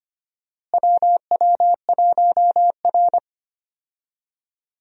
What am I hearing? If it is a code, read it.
Morse code, heard as a keyed tone, about 25 wpm, 700 Hz. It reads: WW1R
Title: USA Call Signs - 4 Characters - Repeated 2x - 25wpm